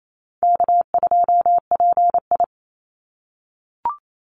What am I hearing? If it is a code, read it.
Morse code, heard as a keyed tone, about 28 wpm, 700 Hz. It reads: K2PI
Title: Easy Call Signs - 28wpm